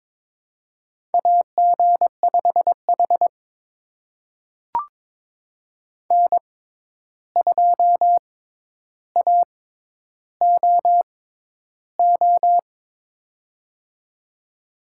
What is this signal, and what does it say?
Morse code, heard as a keyed tone, about 22 wpm, spaced as for 11 wpm, 700 Hz. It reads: AG5H N2AOO
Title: USA Call Signs - 6x Character Spacing - 22wpm